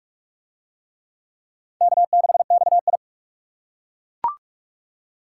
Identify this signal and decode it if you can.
Morse code, heard as a keyed tone, about 40 wpm, 700 Hz. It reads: K6XI